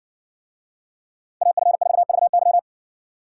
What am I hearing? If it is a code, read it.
Morse code, heard as a keyed tone, about 60 wpm, 700 Hz. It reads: AF4VY